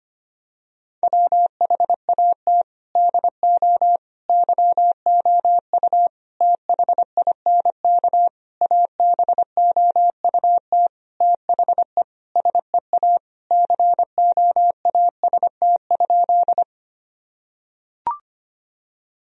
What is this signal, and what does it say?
Morse code, heard as a keyed tone, about 25 wpm, 700 Hz. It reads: WHAT DO YOU THINK ABOUT THE SEA COAST?
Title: Sentences from Top 1000 Words - Review Part 4 - 25wpm